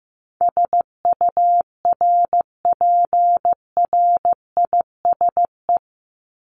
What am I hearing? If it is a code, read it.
Morse code, heard as a keyed tone, about 15 wpm, 700 Hz. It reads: SURPRISE
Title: Top 1751-2000 Words 15wpm